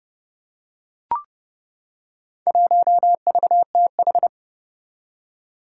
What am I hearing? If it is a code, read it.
Morse code, heard as a keyed tone, about 30 wpm, 700 Hz. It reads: 1VTH
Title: Four Challenging Letter-Number Combinations 30wpm